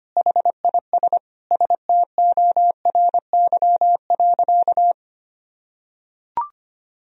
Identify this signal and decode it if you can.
Morse code, heard as a keyed tone, about 25 wpm, 700 Hz. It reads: HIS STORY.
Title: Sentences from Top 1000 Words - Review Part 1 - 25wpm